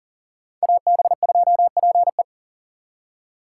Morse code, heard as a keyed tone, about 51 wpm, 700 Hz. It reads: AB2PE